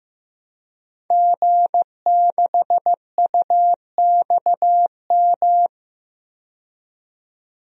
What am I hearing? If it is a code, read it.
Morse code, heard as a keyed tone, about 15 wpm, 700 Hz. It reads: G6UXM
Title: DXCC-100 Call Signs - 5 Characters - 15wpm